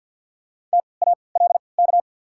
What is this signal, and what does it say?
Morse code, heard as a keyed tone, about 53 wpm, spaced as for 29 wpm, 690 Hz. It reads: TALK
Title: 3x Character Spacing - Top 500 Words - 50wpm